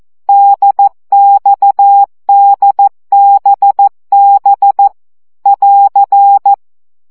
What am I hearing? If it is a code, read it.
Morse code, heard as a keyed tone, about 14 wpm, 800 Hz. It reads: DXDBB <AR>
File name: CW gyakorlas d b x 12 baud